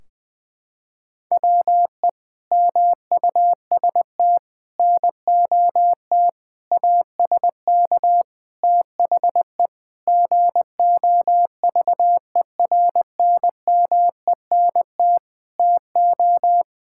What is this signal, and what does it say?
Morse code, heard as a keyed tone, about 21 wpm, 700 Hz. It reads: WE MUST NOT ASK THE GOVERNMENT TO
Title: Sentences from Top 1600 Words 20wpm